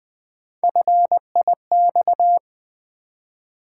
Morse code, heard as a keyed tone, about 20 wpm, 700 Hz. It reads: FIX
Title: Build A Word - 1 Letter At A Time - 5-Letter Words - 20wpm